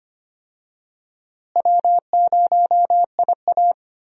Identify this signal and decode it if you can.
Morse code, heard as a keyed tone, about 25 wpm, 700 Hz. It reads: W0IA
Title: USA Call Signs - 4 Characters - Repeated 3x - 25wpm